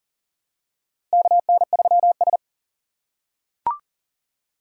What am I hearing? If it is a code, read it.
Morse code, heard as a keyed tone, about 40 wpm, 700 Hz. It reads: KN3S
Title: USA Call Signs - 4 Characters - 6x Character Spacing - 40wpm